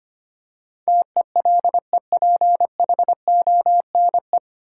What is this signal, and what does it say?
Morse code, heard as a keyed tone, about 25 wpm, 700 Hz. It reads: TELEPHONE
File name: practice / 25wpm / Easy Words with Hints - 25wpm